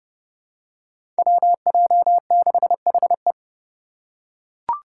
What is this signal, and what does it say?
Morse code, heard as a keyed tone, about 31 wpm, 700 Hz. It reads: WJ6HE
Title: Five Challenging Letter-Number Combinations 30wpm